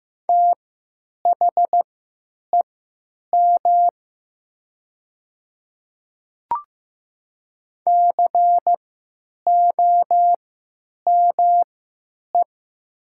Morse code, heard as a keyed tone, about 15 wpm, spaced as for 9 wpm, 700 Hz. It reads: THEM COME
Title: 3x Character Spacing - Top 200 Words - 15wpm